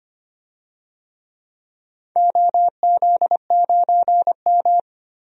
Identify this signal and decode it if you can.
Morse code, heard as a keyed tone, about 25 wpm, 700 Hz. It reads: OZ9M
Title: DXCC-100 Call Signs - 4 Characters - 25wpm